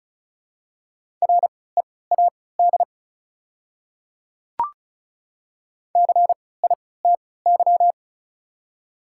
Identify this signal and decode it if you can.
Morse code, heard as a keyed tone, about 34 wpm, spaced as for 20 wpm, 700 Hz. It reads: READ CITY
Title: 3x Character Spacing - Top 300 Words - 35wpm